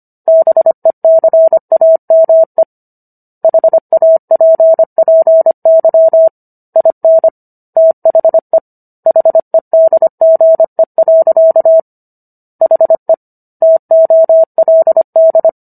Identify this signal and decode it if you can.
Morse code, heard as a keyed tone, about 25 wpm, 650 Hz. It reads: BECAME HAPPY IN THE HEDGE. HE TOLD